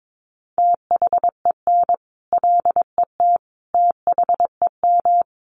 Morse code, heard as a keyed tone, about 22 wpm, 700 Hz. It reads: THEN LET THEM
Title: Sentences from Top 1400 Words 22wpm